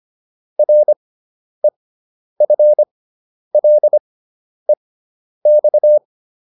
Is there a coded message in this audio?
Yes: Morse code from 0.6 to 6.0 s, heard as a keyed tone, about 25 wpm, spaced as for 10 wpm, 600 Hz. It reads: REFLEX